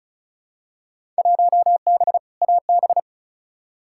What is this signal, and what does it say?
Morse code, heard as a keyed tone, about 36 wpm, 700 Hz. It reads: 1B AB
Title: Field Day Exchanges - 35wpm